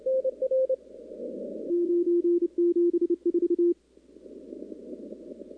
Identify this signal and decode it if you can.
Morse code, heard as a keyed tone, about 30 wpm, 340 Hz. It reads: 974
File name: contests / 2019-cw-wpx-kt5j-k5pi- -2330 0-11